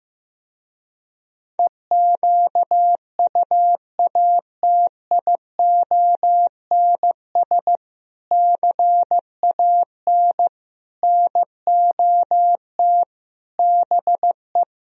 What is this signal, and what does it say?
Morse code, heard as a keyed tone, about 15 wpm, 700 Hz. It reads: EQUATIONS CAN NOT BE
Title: Speed-Racing - Sets of 5 Words 15wpm